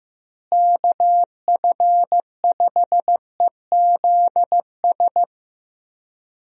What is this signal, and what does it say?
Morse code, heard as a keyed tone, about 15 wpm, 700 Hz. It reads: KF5EZS
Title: USA Call Signs - 6 Characters - Repeated 3x - 15wpm